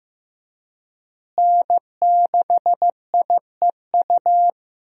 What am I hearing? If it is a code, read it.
Morse code, heard as a keyed tone, about 15 wpm, 700 Hz. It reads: N6IEU
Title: USA Call Signs - 5 Characters - Repeated 2x - 15wpm